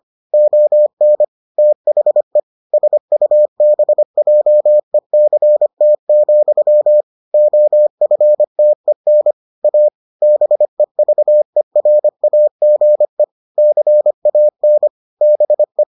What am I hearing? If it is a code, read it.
Morse code, heard as a keyed tone, about 25 wpm, 600 Hz. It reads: ON THE SUBJECT, OFTEN A BEVERAGE CAN BE